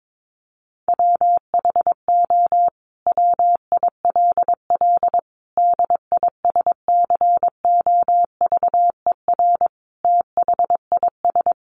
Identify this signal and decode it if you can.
Morse code, heard as a keyed tone, about 22 wpm, 700 Hz. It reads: WHO WILL DISCOVER THIS
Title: Sentences from Top 1100 Words 22wpm